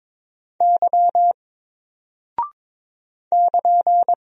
Morse code, heard as a keyed tone, about 22 wpm, 700 Hz. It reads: Y <KN>
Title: Single Character (Letter:Number:Punctuation:Prosign) 22wpm